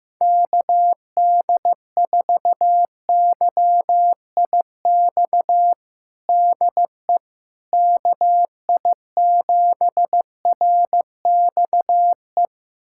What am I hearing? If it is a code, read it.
Morse code, heard as a keyed tone, about 15 wpm, 700 Hz. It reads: KD4YIX DE KI7RXE
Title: USA Callsign DE Callsign - 6 Characters - 15wpm